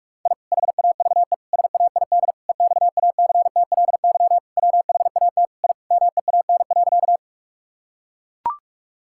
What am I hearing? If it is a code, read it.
Morse code, heard as a keyed tone, about 41 wpm, 700 Hz. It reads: I HAVE SAID EXACTLY WHAT I MEAN.
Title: Sentences from Top 1000 Words - Review Part 6 - 45wpm